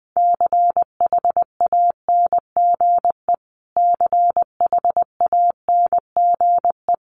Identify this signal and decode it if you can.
Morse code, heard as a keyed tone, about 20 wpm, 700 Hz. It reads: CHANGE CHANGE
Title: Top 500 Words Repeated 20wpm